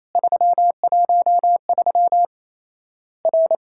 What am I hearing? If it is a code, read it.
Morse code, heard as a keyed tone, about 28 wpm, 700 Hz. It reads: 313 R
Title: Worked All Europe DX Contest - QTC Exchange - 28wpm